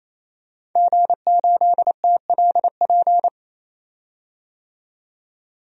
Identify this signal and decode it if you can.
Morse code, heard as a keyed tone, about 28 wpm, 700 Hz. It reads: G8TLP
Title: DXCC-100 Call Signs - 5 Characters - 28wpm